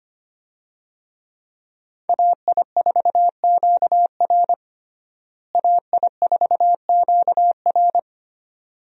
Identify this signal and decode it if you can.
Morse code, heard as a keyed tone, about 25 wpm, 700 Hz. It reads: AI4QR AI4QR
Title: USA Call Signs - 5 Characters - Repeated 2x - 25wpm